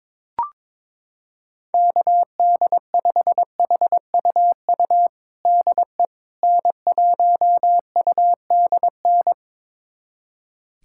Morse code, heard as a keyed tone, about 21 wpm, 700 Hz. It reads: KD5HUU DE N1UDN
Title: USA Callsign DE Callsign 22wpm